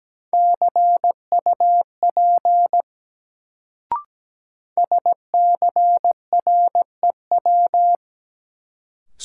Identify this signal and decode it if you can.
Morse code, heard as a keyed tone, about 17 wpm, 700 Hz. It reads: CUP SCREW
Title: Sentences from Top 1600 Words - New Word Drill Alt - 17wpm